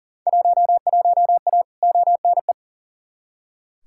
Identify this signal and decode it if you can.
Morse code, heard as a keyed tone, about 40 wpm, 700 Hz. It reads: 11A ONE
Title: Field Day Exchanges Encore - 40wpm